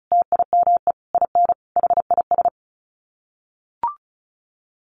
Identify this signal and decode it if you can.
Morse code, heard as a keyed tone, about 34 wpm, 700 Hz. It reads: TIME IN HIS
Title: Sets of 5 Words Encore2 35wpm